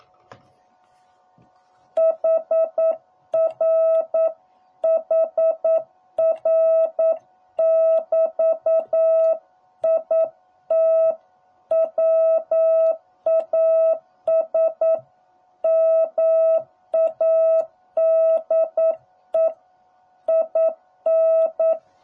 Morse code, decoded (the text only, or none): HR HR <BT> IT WAS MADE IN